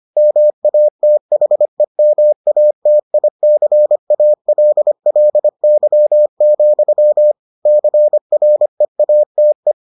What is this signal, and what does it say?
Morse code, heard as a keyed tone, about 25 wpm, 600 Hz. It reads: MATHEMATICALLY, CREATE